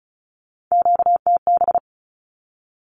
Morse code, heard as a keyed tone, about 36 wpm, 700 Hz. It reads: QTB
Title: Q-Codes with Spoken Meaning - 35wpm